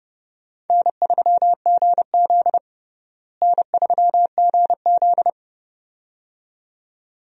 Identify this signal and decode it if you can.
Morse code, heard as a keyed tone, about 32 wpm, 700 Hz. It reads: N3GZ N3GZ
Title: USA Call Signs - 4 Characters - Repeated 2x - 30wpm